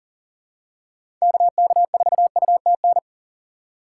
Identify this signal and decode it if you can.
Morse code, heard as a keyed tone, about 40 wpm, 700 Hz. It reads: KK4UTN
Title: USA Call Signs - 4x Character Spacing - 40wpm